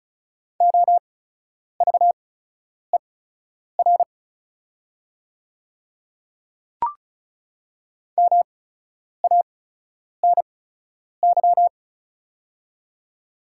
Morse code, heard as a keyed tone, about 36 wpm, spaced as for 10 wpm, 700 Hz. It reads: OVER MANY
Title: 8x Character Spacing - Top 100 Words - 35wpm